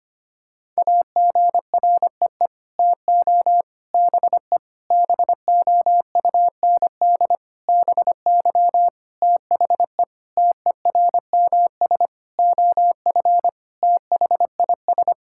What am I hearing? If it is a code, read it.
Morse code, heard as a keyed tone, about 25 wpm, 700 Hz. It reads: AGREE TO BE BOUND BY THE TERMS OF THIS
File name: Sets of 9 Words - 25wpm